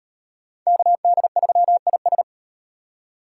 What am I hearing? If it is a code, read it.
Morse code, heard as a keyed tone, about 35 wpm, 700 Hz. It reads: KD3IS